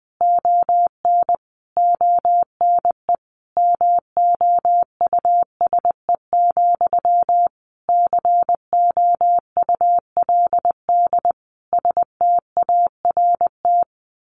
Morse code, heard as a keyed tone, about 20 wpm, 700 Hz. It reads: ON ONE MOUSE, COULD START
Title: A Sound Of Thunder 20wpm